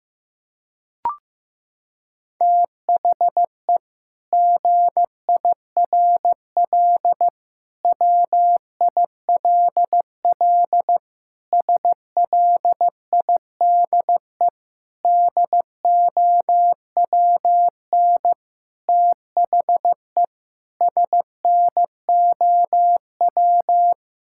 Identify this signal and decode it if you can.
Morse code, heard as a keyed tone, about 15 wpm, 700 Hz. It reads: THE GIRL WILL SLIDE DOWN THE SNOW
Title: Sentences from Top 1600 Words 15wpm